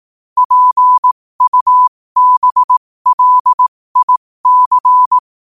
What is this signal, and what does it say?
Morse code, heard as a keyed tone, about 18 wpm, spaced as for 15 wpm, 1000 Hz. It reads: PUBLIC
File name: qotdfiles / 15wpm